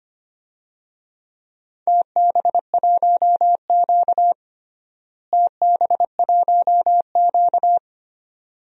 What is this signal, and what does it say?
Morse code, heard as a keyed tone, about 25 wpm, 700 Hz. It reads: TB1Q TB1Q
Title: DXCC-100 Call Signs - 4 Characters - Repeated 2x - 25wpm